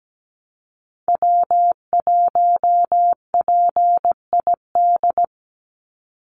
Morse code, heard as a keyed tone, about 17 wpm, 700 Hz. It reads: W1PID
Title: Call Signs Repeated 3x - 17wpm